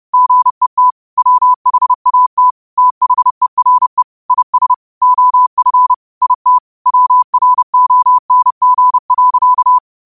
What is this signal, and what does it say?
Morse code, heard as a keyed tone, about 30 wpm, 1000 Hz. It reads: GET WHAT THERE IS OF IT WRONG.
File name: qotdfiles / 30wpm